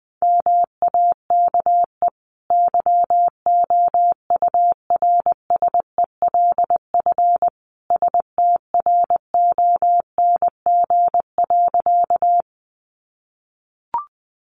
Morse code, heard as a keyed tone, about 19 wpm, 700 Hz. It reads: MAKE YOURSELF STRONG.
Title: Sentences from Top 600 Words 20wpm